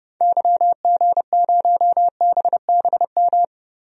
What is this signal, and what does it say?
Morse code, heard as a keyed tone, about 30 wpm, 700 Hz. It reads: YG0BBM
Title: DXCC-100 Call Signs - 6 Characters - Repeated 3x - 30wpm